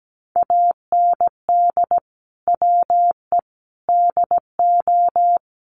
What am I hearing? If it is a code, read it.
Morse code, heard as a keyed tone, about 17 wpm, 700 Hz. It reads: AND WE DO
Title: Sets of 5 Words Encore 17wpm